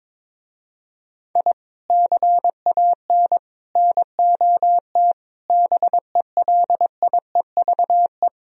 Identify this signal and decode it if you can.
Morse code, heard as a keyed tone, about 23 wpm, 700 Hz. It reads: I CAN NOT BELIEVE
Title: Speed-Racing - Sets of 4 Words 22wpm